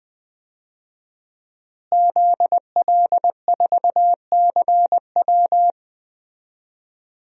Morse code, heard as a keyed tone, about 20 wpm, 700 Hz. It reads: ZL4CW
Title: DXCC-100 Call Signs - 5 Characters - Repeated 2x - 20wpm